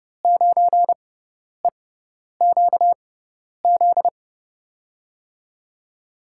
Morse code, heard as a keyed tone, about 31 wpm, spaced as for 10 wpm, 700 Hz. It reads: 9EQZ
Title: USA Call Signs - 6 Characters - 6x Character Spacing - 30wpm